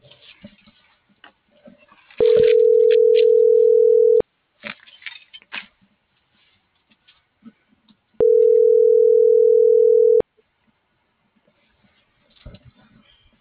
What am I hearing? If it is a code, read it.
Ringback tone (North America), 440 + 480 Hz.